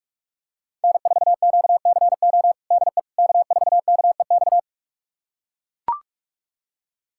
Morse code, heard as a keyed tone, about 41 wpm, 690 Hz. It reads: N4QCO DE K4KEX
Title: USA Callsign DE Callsign 45wpm